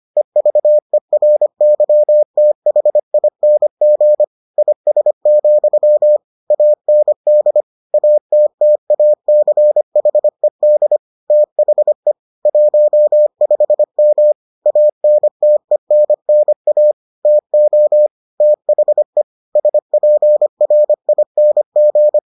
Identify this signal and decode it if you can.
Morse code, heard as a keyed tone, about 25 wpm, 600 Hz. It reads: EVERYTHING IS, AND ATTACHED THE 15M ANTENNA TO THE SPRING